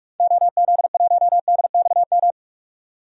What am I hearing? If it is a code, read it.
Morse code, heard as a keyed tone, about 46 wpm, 700 Hz. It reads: OZ1DXM
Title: DXCC-100 Call Signs - 6 Characters - 45wpm